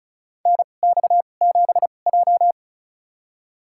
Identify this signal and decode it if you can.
Morse code, heard as a keyed tone, about 36 wpm, spaced as for 25 wpm, 700 Hz. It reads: NX7J
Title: USA Call Signs - 4 Characters - 2x Character Spacing - 35wpm